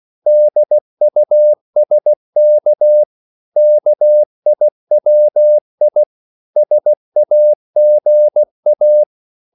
Morse code, heard as a keyed tone, about 16 wpm, 600 Hz. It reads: DUSK KIWI SAGA
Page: Korte danske ord 16wpm